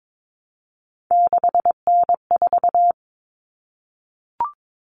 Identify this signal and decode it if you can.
Morse code, heard as a keyed tone, about 22 wpm, 700 Hz. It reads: 6N4
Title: Common Cut Numbers - 3-Digit 22wpm